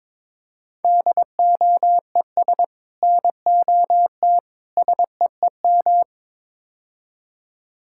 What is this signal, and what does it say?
Morse code, heard as a keyed tone, about 22 wpm, 700 Hz. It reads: DOES NOT SEEM
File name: practice / 22wpm / Build A Phrase - 5-Word Phrases - 22wpm